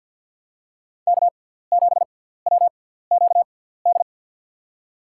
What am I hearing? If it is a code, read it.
Morse code, heard as a keyed tone, about 49 wpm, spaced as for 17 wpm, 700 Hz. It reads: K7WQD
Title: USA Call Signs - 5 Characters - 6x Character Spacing - 50wpm